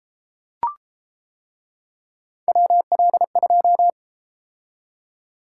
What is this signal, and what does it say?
Morse code, heard as a keyed tone, about 30 wpm, 700 Hz. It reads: WL2